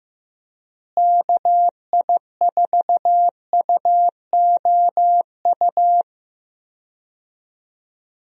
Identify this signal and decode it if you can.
Morse code, heard as a keyed tone, about 15 wpm, 700 Hz. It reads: KI4UOU
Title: USA Call Signs - 6 Characters - Repeated 3x - 15wpm